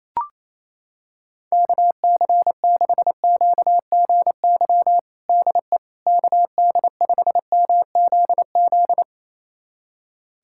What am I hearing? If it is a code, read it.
Morse code, heard as a keyed tone, about 28 wpm, 700 Hz. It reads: KC6QGY DE KD5MZZ
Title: USA Callsign DE Callsign - 6 Characters - 28wpm